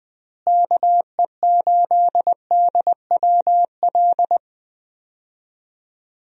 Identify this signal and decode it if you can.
Morse code, heard as a keyed tone, about 21 wpm, 700 Hz. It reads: KE8DWL
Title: USA Call Signs - 6 Characters - Repeated 3x - 20wpm